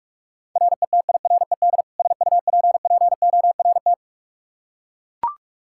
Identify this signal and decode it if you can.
Morse code, heard as a keyed tone, about 41 wpm, 700 Hz. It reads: RETIRED SUPPORT
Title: Practice QSO - Exchange 3 - More Info - 45wpm